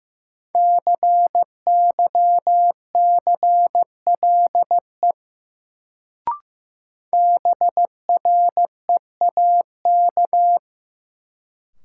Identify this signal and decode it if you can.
Morse code, heard as a keyed tone, about 15 wpm, 700 Hz. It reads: CYCLE BREAK
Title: Sentences from Top 900 Words - New Word Drill - 15wpm